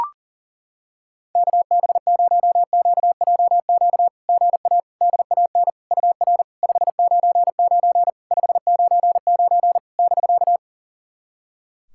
Morse code, heard as a keyed tone, about 39 wpm, 700 Hz. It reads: KB0QJQ GA DAN UR 599 599 <BK>
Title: SOTA QSO Practice 40wpm